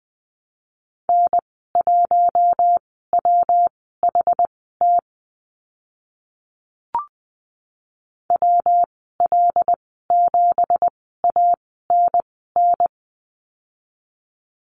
Morse code, heard as a keyed tone, about 20 wpm, spaced as for 15 wpm, 700 Hz. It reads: N1WHT WL7ANN
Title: 167 Lesson 27g - Number 7 - Call Signs 20wpm